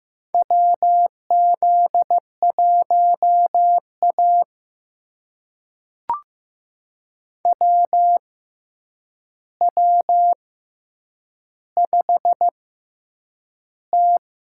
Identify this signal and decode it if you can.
Morse code, heard as a keyed tone, about 15 wpm, spaced as for 8 wpm, 700 Hz. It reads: WZ1A WW5T
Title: USA Call Signs - 6x Character Spacing - 15wpm